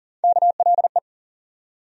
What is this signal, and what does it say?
Morse code, heard as a keyed tone, about 40 wpm, 700 Hz. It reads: KLE
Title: USA Call Sign Postfixes - 3 Letters 40wpm